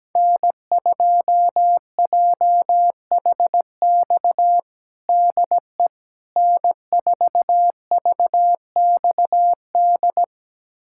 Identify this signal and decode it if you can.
Morse code, heard as a keyed tone, about 17 wpm, 700 Hz. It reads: N2JHX DE N4VXD
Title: USA Callsign DE Callsign - 5 Characters - 17wpm